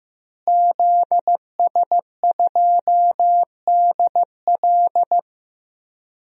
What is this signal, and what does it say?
Morse code, heard as a keyed tone, about 15 wpm, 700 Hz. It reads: ZS2DL
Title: DXCC-100 Call Signs - 5 Characters - Repeated 2x - 15wpm